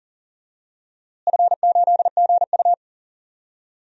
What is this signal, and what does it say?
Morse code, heard as a keyed tone, about 40 wpm, 700 Hz. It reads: F8GU
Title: DXCC-100 Call Signs - 4 Characters - Repeated 2x - 40wpm